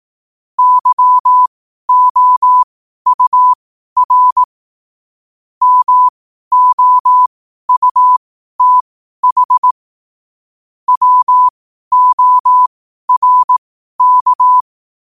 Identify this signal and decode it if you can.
Morse code, heard as a keyed tone, about 18 wpm, spaced as for 12 wpm, 1000 Hz. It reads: YOUR MOUTH WORK
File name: qotdfiles / 12wpm